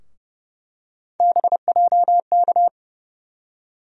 Morse code, heard as a keyed tone, about 30 wpm, 700 Hz. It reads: BJK
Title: USA Call Sign Postfixes - 3 Letters 30wpm